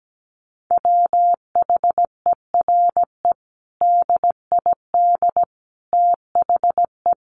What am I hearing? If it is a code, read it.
Morse code, heard as a keyed tone, about 17 wpm, 700 Hz. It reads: WHERE DID THE
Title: Sentences from Top 1100 Words 17wpm